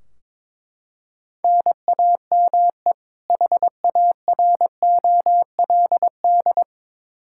Morse code, heard as a keyed tone, about 22 wpm, 700 Hz. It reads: NAME HAROLD
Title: Practice QSO - Exchange 1 - Beginning - 22wpm